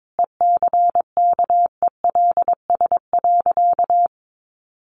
Morse code, heard as a keyed tone, about 22 wpm, 700 Hz. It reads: ECKELS.
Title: A Sound Of Thunder 22wpm